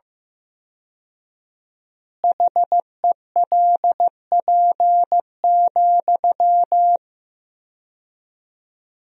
Morse code, heard as a keyed tone, about 15 wpm, 700 Hz. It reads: HELP,
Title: 4x Word-Spacing - Sentences from Top 300 Words - 15wpm